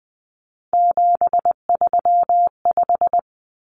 Morse code, heard as a keyed tone, about 20 wpm, 700 Hz. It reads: 735
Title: Common Cut Numbers - 3-Digit 20wpm